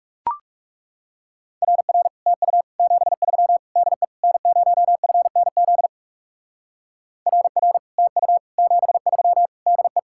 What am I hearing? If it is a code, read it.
Morse code, heard as a keyed tone, about 36 wpm, 700 Hz. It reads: RR TU 73 DE N0FNZ RR TU 73 DE
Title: Speed-Racing Common QSO Phrases Encore - 30wpm